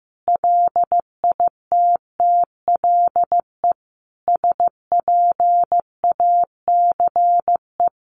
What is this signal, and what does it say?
Morse code, heard as a keyed tone, about 15 wpm, 700 Hz. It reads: LITTLE SPACE
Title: Sentences from Top 1500 Words 15wpm